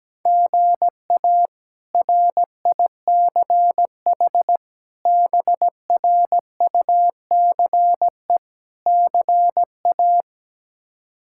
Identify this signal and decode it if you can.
Morse code, heard as a keyed tone, about 17 wpm, 700 Hz. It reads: GA RICH BRUCE CA
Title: K1USN SST QSO Practice - 17wpm